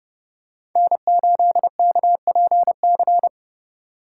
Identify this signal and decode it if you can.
Morse code, heard as a keyed tone, about 29 wpm, 700 Hz. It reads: N8KPC